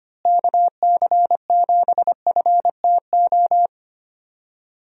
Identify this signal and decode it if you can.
Morse code, heard as a keyed tone, about 25 wpm, 700 Hz. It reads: KC7FTO
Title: USA Call Signs - 6 Characters - Repeated 2x - 25wpm